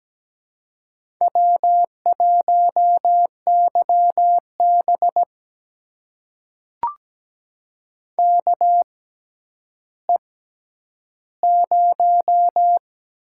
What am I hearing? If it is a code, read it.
Morse code, heard as a keyed tone, about 17 wpm, spaced as for 10 wpm, 700 Hz. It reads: W1YB KE0
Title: USA Call Signs - 6x Character Spacing - 17wpm